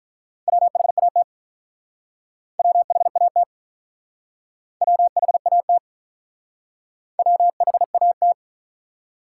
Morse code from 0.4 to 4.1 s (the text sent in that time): WHAT WHAT